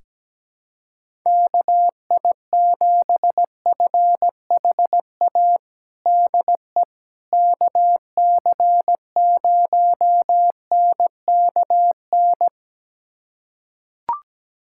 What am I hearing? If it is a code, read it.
Morse code, heard as a keyed tone, about 17 wpm, 700 Hz. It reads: KI7FHA DE KC0NKN
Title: USA Callsign DE Callsign 17wpm